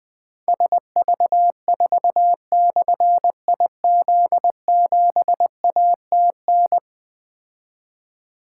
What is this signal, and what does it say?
Morse code, heard as a keyed tone, about 21 wpm, 700 Hz. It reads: SV4/IZ7ATN
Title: Difficult Call Signs 20wpm